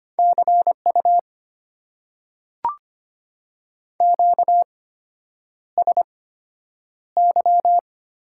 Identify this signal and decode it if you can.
Morse code, heard as a keyed tone, about 25 wpm, spaced as for 9 wpm, 700 Hz. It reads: CU QSY
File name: Top 90 QSO Elements - 8x Character Spacing - 25wpm